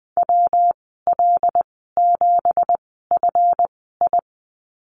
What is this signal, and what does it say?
Morse code, heard as a keyed tone, about 20 wpm, spaced as for 15 wpm, 700 Hz. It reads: WL7FI
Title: 186 Lesson 30g - Letter b - Call Signs 20wpm